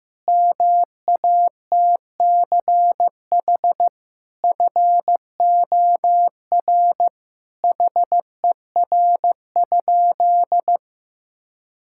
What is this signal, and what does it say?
Morse code, heard as a keyed tone, about 15 wpm, 700 Hz. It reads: MATCH FOR HER?
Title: Sentences from Top 1600 Words 15wpm